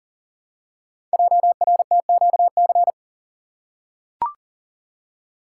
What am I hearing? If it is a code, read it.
Morse code, heard as a keyed tone, about 39 wpm, 700 Hz. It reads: JRTQC
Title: Code Groups - 5 Letters - 40wpm